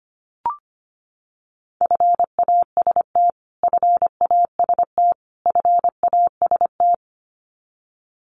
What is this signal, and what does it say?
Morse code, heard as a keyed tone, about 25 wpm, 700 Hz. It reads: FAST FAST FAST
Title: Top 500 Words Repeated 25wpm